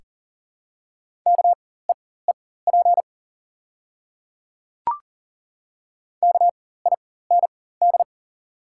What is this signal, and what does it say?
Morse code, heard as a keyed tone, about 39 wpm, spaced as for 19 wpm, 700 Hz. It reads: KEEP KIND
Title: 4x Character Spacing - Top 500 Words - 40wpm